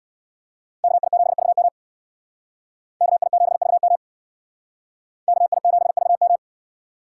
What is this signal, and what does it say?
Morse code, heard as a keyed tone, about 62 wpm, 700 Hz. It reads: KI7VM KI7VM KI7VM
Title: Speed Racing - USA Call Signs 50wpm